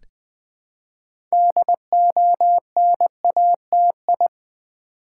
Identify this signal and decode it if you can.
Morse code, heard as a keyed tone, about 20 wpm, 700 Hz. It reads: DONATI
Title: Build A Word - 3 Letters At A Time - 8-Letter Words - 20wpm